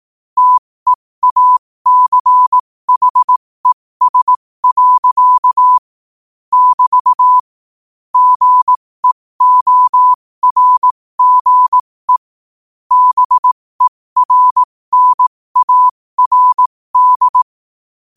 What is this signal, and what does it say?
Morse code, heard as a keyed tone, about 18 wpm, spaced as for 15 wpm, 1000 Hz. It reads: TEACHES. <BT> GEORGE BERNARD
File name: qotdfiles / 15wpm